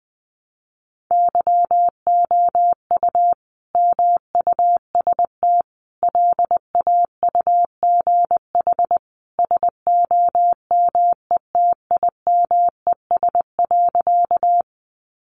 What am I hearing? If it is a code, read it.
Morse code, heard as a keyed tone, about 20 wpm, 700 Hz. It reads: YOU MUST LAUGH SOMETIMES.
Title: Sentences from Top 600 Words 20wpm